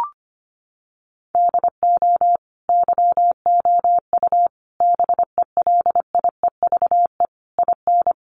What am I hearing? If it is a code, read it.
Morse code, heard as a keyed tone, about 25 wpm, 700 Hz. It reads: DO YOU BELIEVE IN
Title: Sentences from Top 1200 Words 25wpm